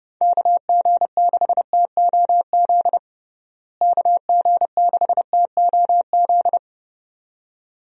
Morse code, heard as a keyed tone, about 30 wpm, 700 Hz. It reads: KG6TOZ KG6TOZ
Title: USA Call Signs - 6 Characters - Repeated 2x - 30wpm